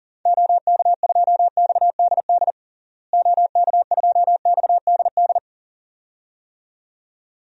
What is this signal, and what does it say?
Morse code, heard as a keyed tone, about 40 wpm, 700 Hz. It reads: OK2XDD OK2XDD
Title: DXCC-100 Call Signs - 6 Characters - Repeated 2x - 40wpm